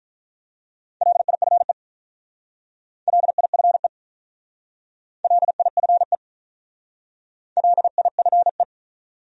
Morse code, heard as a keyed tone, about 44 wpm, 700 Hz. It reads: LIFE LIFE LIFE LIFE